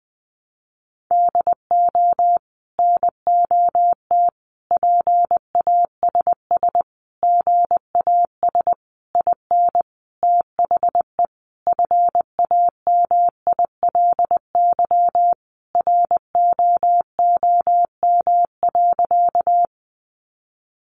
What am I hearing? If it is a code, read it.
Morse code, heard as a keyed tone, about 20 wpm, 700 Hz. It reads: DO NOT PASS GAS IN THE FAMILY ROOM.
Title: Sentences from Top 1100 Words 20wpm